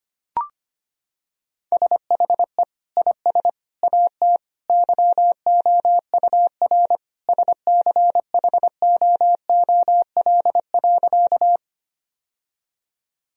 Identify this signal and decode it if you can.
Morse code, heard as a keyed tone, about 25 wpm, 700 Hz. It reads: SHE IS AT YOUR SCHOOL.
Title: Sentences from Top 1000 Words - Review Part 1 - 25wpm